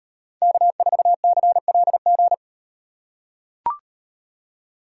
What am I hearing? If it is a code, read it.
Morse code, heard as a keyed tone, about 38 wpm, 700 Hz. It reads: K4CLG